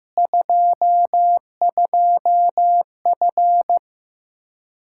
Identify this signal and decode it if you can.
Morse code, heard as a keyed tone, about 15 wpm, 700 Hz. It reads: 22F
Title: Practice QSO - Exchange 2 - Info - 15wpm